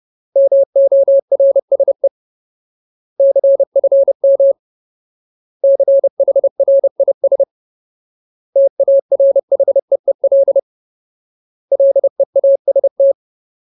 Morse code, heard as a keyed tone, about 30 wpm, 550 Hz. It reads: MORSE CFM CHRIS TARHEEL LEAST
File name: Ultimate Word List / 30WPM Ultimate